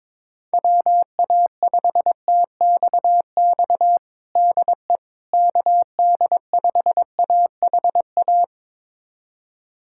Morse code, heard as a keyed tone, about 22 wpm, 700 Hz. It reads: WA5TXX DE KD5AHA
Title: USA Callsign DE Callsign - 6 Characters - 22wpm